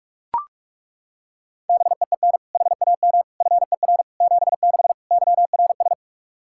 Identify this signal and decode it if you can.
Morse code, heard as a keyed tone, about 41 wpm, 700 Hz. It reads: BEEN HAM FER 76 YRS
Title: K1USN SST QSO Practice 45wpm